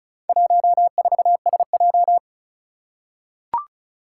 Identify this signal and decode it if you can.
Morse code, heard as a keyed tone, about 36 wpm, 700 Hz. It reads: 14SJ
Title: Four Challenging Letter-Number Combinations 35wpm